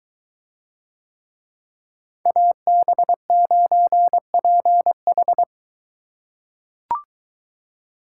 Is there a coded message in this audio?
Yes: Morse code from 2.3 to 5.4 s, heard as a keyed tone, about 21 wpm, 700 Hz. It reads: AB9PH